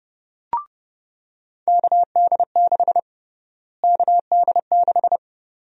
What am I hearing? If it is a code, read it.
Morse code, heard as a keyed tone, about 31 wpm, 700 Hz. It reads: KD6 KD6
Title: USA Call Sign Prefixes - Repeated 2x - 30wpm